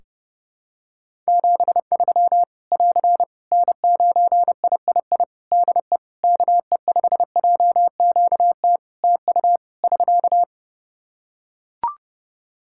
Morse code, heard as a keyed tone, about 30 wpm, 700 Hz. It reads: 73 <AR> N9III DE KE5JQT TU <SK>
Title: Practice QSO - Exchange 4 - Ending - 30wpm